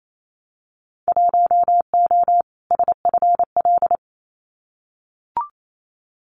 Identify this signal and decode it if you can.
Morse code, heard as a keyed tone, about 28 wpm, 700 Hz. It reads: 1O SFL
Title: Winter Field Day Exchanges 28wpm